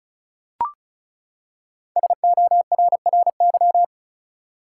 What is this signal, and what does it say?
Morse code, heard as a keyed tone, about 35 wpm, 700 Hz. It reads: SORRY
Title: Five Letter Words - 35wpm